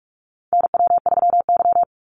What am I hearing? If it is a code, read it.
Morse code, heard as a keyed tone, about 44 wpm, 700 Hz. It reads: NW3Y
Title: Call Signs - 45wpm